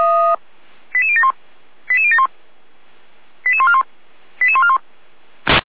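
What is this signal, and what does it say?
Five-tone selective call (ZVEI), heard as quick runs of five tones, tones of about 70 ms. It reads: 88071 88071 88141 88141
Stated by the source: ZVEI five-tone selective call received with an SDR.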